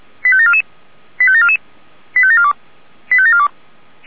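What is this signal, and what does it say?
Five-tone selective call (ZVEI), heard as quick runs of five tones, tones of about 70 ms. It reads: 85744 85744 85742 85742
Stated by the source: ZVEI five-tone call received by SDR, archived by Feuerwehr Orlamünde.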